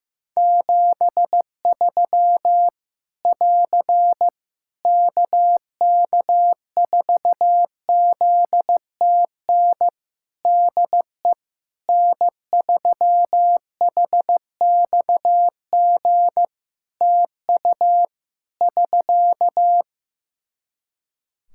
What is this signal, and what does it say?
Morse code, heard as a keyed tone, about 15 wpm, 700 Hz. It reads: 73 <AR> KK4ZTN DE N3HXG TU <SK>
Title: Practice QSO - Exchange 4 - Ending - 15wpm